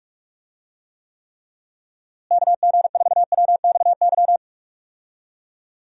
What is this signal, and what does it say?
Morse code, heard as a keyed tone, about 46 wpm, 700 Hz. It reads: KG4WXY
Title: USA Call Signs - 6 Characters - Repeated 3x - 45wpm